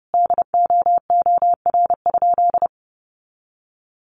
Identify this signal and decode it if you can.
Morse code, heard as a keyed tone, about 30 wpm, 700 Hz. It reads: DOOR?
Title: Sentences from Top 900 Words 30wpm